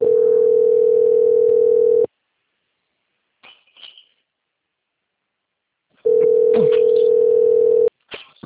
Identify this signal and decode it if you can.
Ringback tone (North America), 440 + 480 Hz.